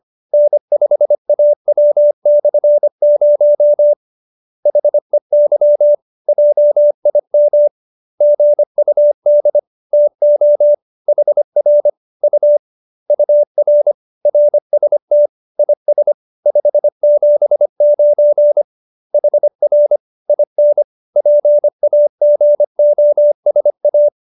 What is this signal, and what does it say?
Morse code, heard as a keyed tone, about 25 wpm, 590 Hz. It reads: N5AW/0 HEY JIM GUD TO HR U UR RST IS 579 HR IN PAGOSA
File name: QSO312 25